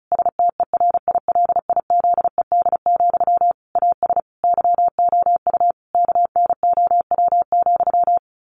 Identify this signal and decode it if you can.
Morse code, heard as a keyed tone, about 34 wpm, 700 Hz. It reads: STERILIZED, AS YOU KNOW,
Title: A Sound Of Thunder 35wpm